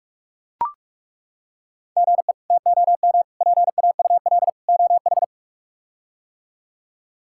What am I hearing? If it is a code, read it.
Morse code, heard as a keyed tone, about 41 wpm, 700 Hz. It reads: GE TOM PAUL OH
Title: K1USN SST QSO Practice - 45wpm